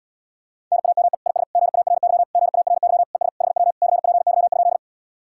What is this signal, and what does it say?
Morse code, heard as a keyed tone, about 78 wpm, 700 Hz. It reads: NAME IS GARY GARY ES HW CPY?